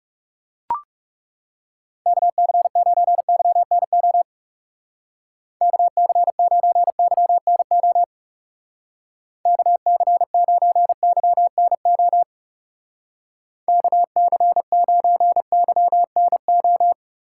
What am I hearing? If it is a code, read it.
Morse code, heard as a keyed tone, about 38 wpm, 700 Hz. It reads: KC9YNO KC9YNO KC9YNO KC9YNO